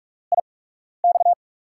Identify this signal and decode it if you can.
Morse code, heard as a keyed tone, about 46 wpm, spaced as for 12 wpm, 700 Hz. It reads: IX
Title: USA Call Signs - 5 Characters - 8x Character Spacing - 45wpm